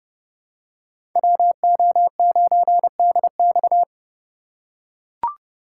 Morse code, heard as a keyed tone, about 30 wpm, 700 Hz. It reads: WO9DX
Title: USA Call Signs - 5 Characters - 4x Character Spacing - 30wpm